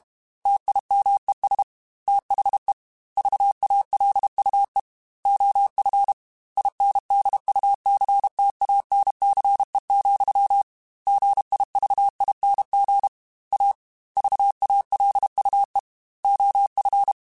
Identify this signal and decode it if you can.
Morse code, heard as a keyed tone, about 32 wpm, 800 Hz. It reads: TIMES THE VALUE OF INDUCTANCE, GIVING A VALUE OF
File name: Technical Subjects / Technical Subjects-mp3s / XL @30